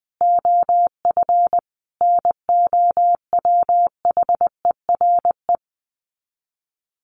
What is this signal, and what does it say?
Morse code, heard as a keyed tone, about 20 wpm, 700 Hz. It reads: OF NOWHERE
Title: Sets of 5 Words Encore 20wpm